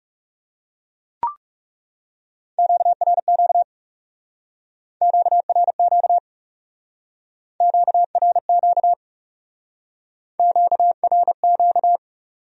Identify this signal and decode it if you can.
Morse code, heard as a keyed tone, about 39 wpm, 700 Hz. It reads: QRQ QRQ QRQ QRQ